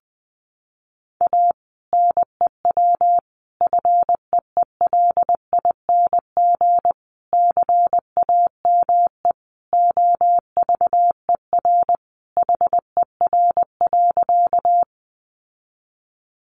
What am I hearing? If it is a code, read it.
Morse code, heard as a keyed tone, about 20 wpm, 700 Hz. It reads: A NEW FEELING CAME OVER HER.
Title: Sentences from Top 1200 Words 20wpm